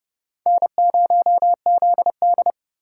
Morse code, heard as a keyed tone, about 29 wpm, 700 Hz. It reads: N0ZD